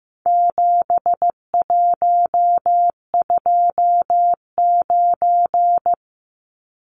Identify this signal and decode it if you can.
Morse code, heard as a keyed tone, about 15 wpm, 700 Hz. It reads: 7129
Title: Numbers - 4-Digit 15wpm